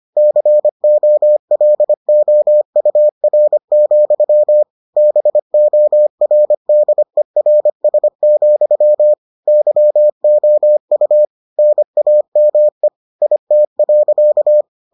Morse code, heard as a keyed tone, about 25 wpm, 600 Hz. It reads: COLOUR, BORDERS, YOU NAME IT.